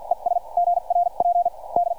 Morse code, heard as a keyed tone, about 46 wpm, spaced as for 38 wpm, 690 Hz. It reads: ES GRP D